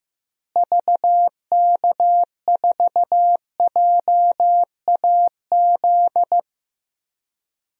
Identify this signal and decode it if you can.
Morse code, heard as a keyed tone, about 15 wpm, 700 Hz. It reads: VK4JAZ
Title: DXCC-100 Call Signs - 6 Characters - Repeated 3x - 15wpm